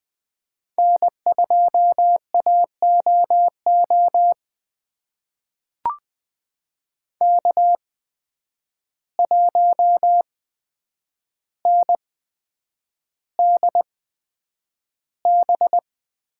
Morse code, heard as a keyed tone, about 20 wpm, spaced as for 9 wpm, 700 Hz. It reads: N2AOO K1NDB
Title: USA Call Signs - 5 Characters - 8x Character Spacing - 20wpm